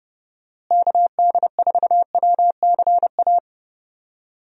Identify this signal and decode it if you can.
Morse code, heard as a keyed tone, about 30 wpm, 700 Hz. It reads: KD4WCA
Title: USA Call Signs - 6 Characters - Repeated 2x - 30wpm